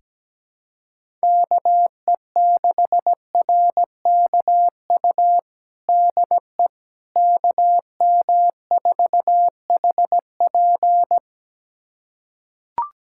Morse code, heard as a keyed tone, about 17 wpm, 700 Hz. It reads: KE6RKU DE KM4HP
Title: USA Callsign DE Callsign 17wpm